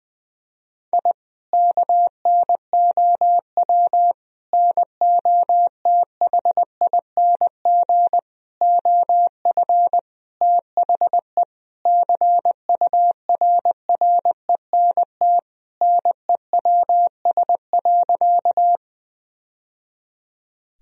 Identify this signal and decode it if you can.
Morse code, heard as a keyed tone, about 20 wpm, 700 Hz. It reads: I KNOW NOTHING OF THE CURRENT NEWS.
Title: Sentences from Top 1000 Words - Review Part 4 - 20wpm